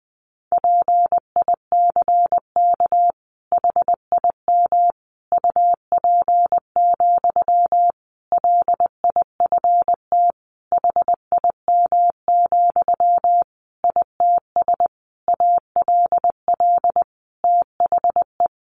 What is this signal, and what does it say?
Morse code, heard as a keyed tone, about 20 wpm, 700 Hz. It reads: PICK HIM UP, LIFT HIM, ITS ALL THE
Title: Sentences from Top 1000 Words 20wpm